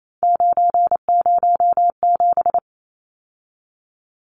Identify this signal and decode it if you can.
Morse code, heard as a keyed tone, about 28 wpm, 700 Hz. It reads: 907
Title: Numbers - 3-Digit 28wpm